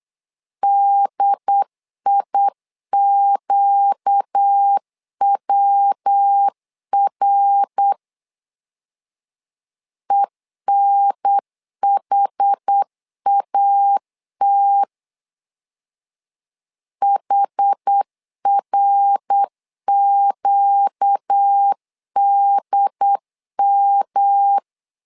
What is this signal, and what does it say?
Morse code, heard as a keyed tone, about 8 wpm, 800 Hz. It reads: DIQWR ENHAT HRQDM
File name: 4 Q 8wpm